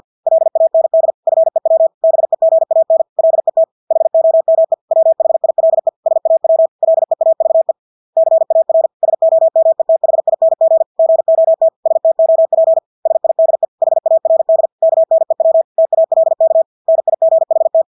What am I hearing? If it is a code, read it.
Morse code, heard as a keyed tone, about 50 wpm, 650 Hz. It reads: LAND FEW BEGAN LET HOME WHILE SAW LEAVE CAR SOMETHING GOT STOP SIDE HARD KNEW TALK NIGHT
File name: Engelske ord 50 wpm